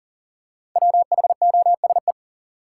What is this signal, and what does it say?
Morse code, heard as a keyed tone, about 40 wpm, 700 Hz. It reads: WHOSE
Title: Five Letter Words - 40wpm